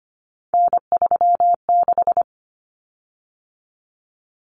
Morse code, heard as a keyed tone, about 25 wpm, 700 Hz. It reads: N36
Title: Common Cut Numbers - 3-Digit 25wpm